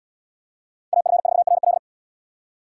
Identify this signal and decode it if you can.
Morse code, heard as a keyed tone, about 77 wpm, 700 Hz. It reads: AF4RL